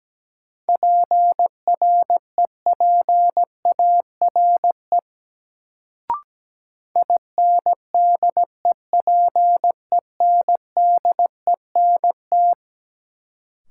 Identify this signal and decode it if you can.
Morse code, heard as a keyed tone, about 17 wpm, 700 Hz. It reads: PREPARE INDEPENDENT
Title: Sentences from Top 900 Words - New Word Drill - 17wpm